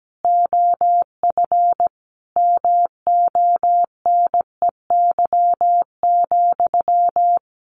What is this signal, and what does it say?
Morse code, heard as a keyed tone, about 17 wpm, 700 Hz. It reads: OF MONEY,
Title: Sentences from Top 900 Words 17wpm